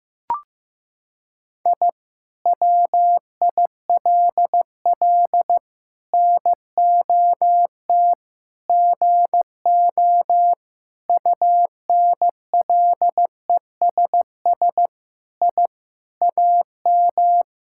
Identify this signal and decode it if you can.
Morse code, heard as a keyed tone, about 15 wpm, 700 Hz. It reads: I WILL NOT GO UNLESS I AM
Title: Sentences from Top 1000 Words - Review Part 6 - 15wpm